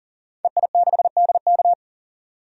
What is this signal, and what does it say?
Morse code, heard as a keyed tone, about 40 wpm, 700 Hz. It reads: EI6DK
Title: DXCC-100 Call Signs - 5 Characters - 40wpm